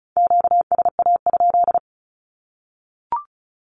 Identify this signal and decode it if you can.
Morse code, heard as a keyed tone, about 34 wpm, 700 Hz. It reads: QSA?
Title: Q-Codes 35wpm